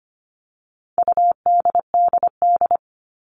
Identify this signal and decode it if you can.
Morse code, heard as a keyed tone, about 25 wpm, 700 Hz. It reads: UDDD
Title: All Cut Numbers - 4-Digit 25wpm